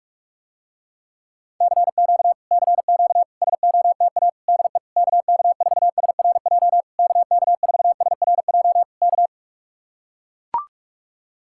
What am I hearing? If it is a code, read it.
Morse code, heard as a keyed tone, about 45 wpm, 700 Hz. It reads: CQ CQ SOTA DE KK4SRJ KK4SRJ K